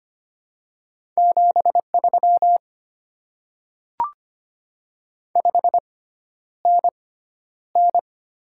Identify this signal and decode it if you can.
Morse code, heard as a keyed tone, about 25 wpm, spaced as for 11 wpm, 700 Hz. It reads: 73 5NN
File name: Top 20 QSO Elements - 6x Character Spacing - 25wpm